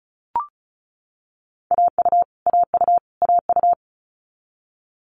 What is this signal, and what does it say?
Morse code, heard as a keyed tone, about 36 wpm, 700 Hz. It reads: AU AU AU
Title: Multi-letter Phonemes Repeated 3x 35wpm